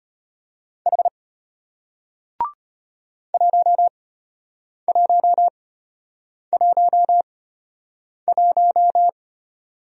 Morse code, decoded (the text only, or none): H 1111